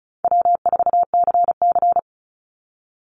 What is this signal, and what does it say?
Morse code, heard as a keyed tone, about 34 wpm, 700 Hz. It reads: W4CC
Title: Easy Call Signs Repeated 3x - 35wpm